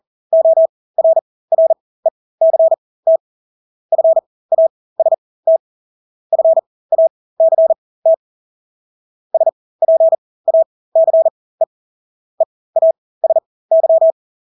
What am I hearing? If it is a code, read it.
Morse code, heard as a keyed tone, about 40 wpm, spaced as for 20 wpm, 650 Hz. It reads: ORRECT FAST FACT SPACE EASY
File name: Engelske ord 40-20 wpm